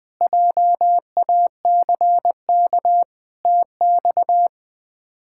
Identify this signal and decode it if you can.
Morse code, heard as a keyed tone, about 20 wpm, 700 Hz. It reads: JACK TX
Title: K1USN SST QSO Practice - 20wpm